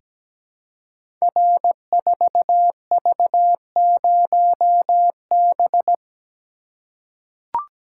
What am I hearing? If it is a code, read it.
Morse code, heard as a keyed tone, about 17 wpm, 700 Hz. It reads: R4V0B
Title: Code Groups - 5 Letter:Numbers - 17wpm